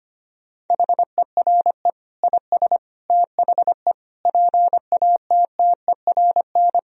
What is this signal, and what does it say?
Morse code, heard as a keyed tone, about 25 wpm, 700 Hz. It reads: HERE IS THE PATTERN
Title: Sentences from Top 1000 Words - Review Part 2 - 25wpm